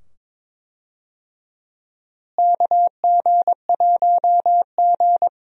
Morse code, heard as a keyed tone, about 22 wpm, 700 Hz. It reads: KG1G
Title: USA Call Signs - 4 Characters - Repeated 2x - 22wpm